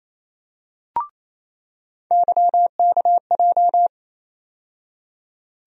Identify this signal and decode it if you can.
Morse code, heard as a keyed tone, about 28 wpm, 700 Hz. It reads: YKJ
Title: USA Call Sign Postfixes - 3 Letters 28wpm